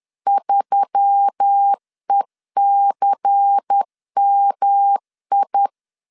Morse code, heard as a keyed tone, about 10 wpm, 800 Hz. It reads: 3ECMI
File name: LEZIONI / 6 PKX0 / 6 PKX0 10wpm 2707